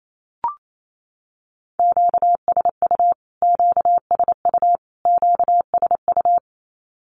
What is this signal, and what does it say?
Morse code, heard as a keyed tone, about 28 wpm, 700 Hz. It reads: QSU QSU QSU
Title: Q-Codes Repeated 3x 28wpm